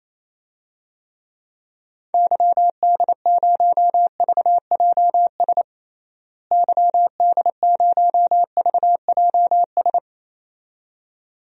Morse code, heard as a keyed tone, about 28 wpm, 700 Hz. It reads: YD0VJS YD0VJS
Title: DXCC-100 Call Signs - 6 Characters - Repeated 2x - 28wpm